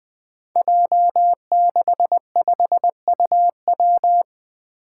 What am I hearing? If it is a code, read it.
Morse code, heard as a keyed tone, about 20 wpm, 700 Hz. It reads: J65UW
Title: Five Challenging Letter-Number Combinations 20wpm